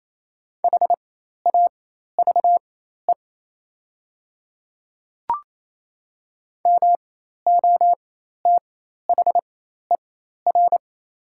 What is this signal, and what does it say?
Morse code, heard as a keyed tone, about 28 wpm, spaced as for 13 wpm, 690 Hz. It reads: HAVE MOTHER
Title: 4x Character Spacing - Top 500 Words - 28wpm